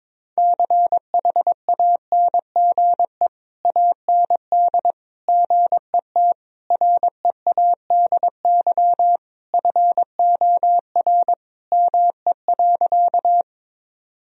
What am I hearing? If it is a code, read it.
Morse code, heard as a keyed tone, about 22 wpm, 700 Hz. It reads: CHANGE AND GET READY FOR ME.
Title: Sentences from Top 1000 Words - Review Part 1 - 22wpm